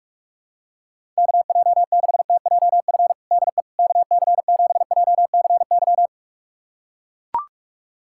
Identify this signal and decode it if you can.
Morse code, heard as a keyed tone, about 41 wpm, 700 Hz. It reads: KJ6TJF DE KC7JCY
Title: USA Callsign DE Callsign - 6 Characters - 45wpm